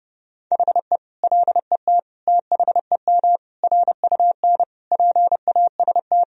Morse code, heard as a keyed tone, about 30 wpm, 700 Hz. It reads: HE LET THEM RUN PAST
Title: Sentences from Top 1000 Words - Review Part 3 - 30wpm